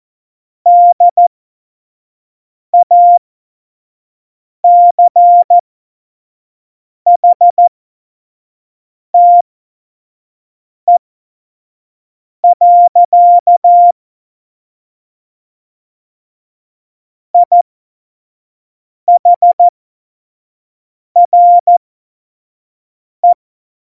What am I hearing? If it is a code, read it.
Morse code, heard as a keyed tone, about 14 wpm, spaced as for 5 wpm, 700 Hz. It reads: DACHTE. IHRE
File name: 25 F70